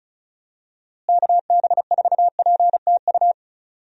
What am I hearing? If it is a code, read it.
Morse code, heard as a keyed tone, about 36 wpm, 700 Hz. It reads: KB4PTU
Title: USA Call Signs - 6 Characters - Repeated 2x - 35wpm